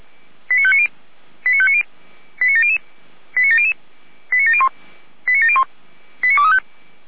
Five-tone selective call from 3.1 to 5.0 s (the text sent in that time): 89700 89701